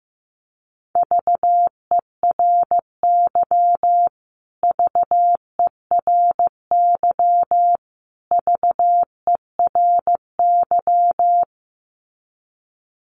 Morse code, heard as a keyed tone, about 15 wpm, 700 Hz. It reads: VERY VERY VERY
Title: Top 500 Words Repeated 15wpm